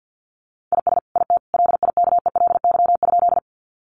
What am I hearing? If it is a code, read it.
Morse code, heard as a keyed tone, about 50 wpm, 700 Hz. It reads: IS IT LIKELY?
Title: Sentences from Top 600 Words 50wpm